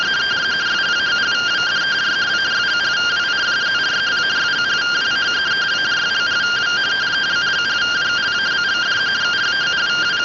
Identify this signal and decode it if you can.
Two-tone teleprinter signal, about 170 Hz shift.